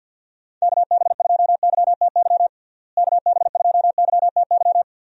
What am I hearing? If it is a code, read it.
Morse code, heard as a keyed tone, about 51 wpm, 700 Hz. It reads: KB2YTY KB2YTY
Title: USA Call Signs - 6 Characters - Repeated 3x - 50wpm